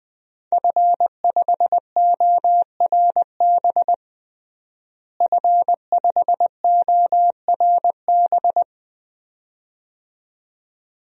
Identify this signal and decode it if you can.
Morse code, heard as a keyed tone, about 20 wpm, 700 Hz. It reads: F5ORB F5ORB
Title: DXCC-100 Call Signs - 5 Characters - Repeated 2x - 20wpm